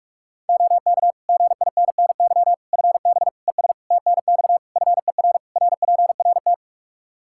Morse code, heard as a keyed tone, about 46 wpm, 700 Hz. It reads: OK GINNY FB ES TNX FER RPRT
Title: Practice QSO - Exchange 2 - Info - 45wpm